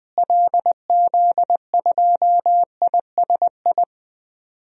Morse code, heard as a keyed tone, about 20 wpm, 700 Hz. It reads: LZ2ISI
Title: DXCC-100 Call Signs - 6 Characters - Repeated 3x - 20wpm